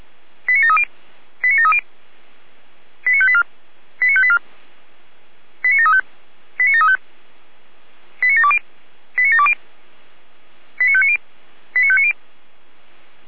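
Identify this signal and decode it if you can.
Five-tone selective call (ZVEI), heard as quick runs of five tones, tones of about 70 ms. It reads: 89730 89730 89584 89584 89735 89735 89720 89720 89590 89590
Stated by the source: ZVEI five-tone selective call 89752 received via SDR.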